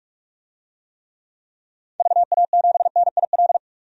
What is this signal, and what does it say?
Morse code, heard as a keyed tone, about 45 wpm, 700 Hz. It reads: VA7NIL
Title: DXCC-100 Call Signs - 6 Characters - 45wpm